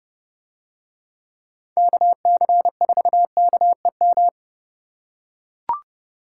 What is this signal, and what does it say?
Morse code, heard as a keyed tone, about 30 wpm, 700 Hz. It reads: KC4KEM